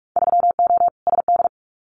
Morse code, heard as a keyed tone, about 46 wpm, 700 Hz. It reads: 3O SD
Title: Winter Field Day Exchanges 45wpm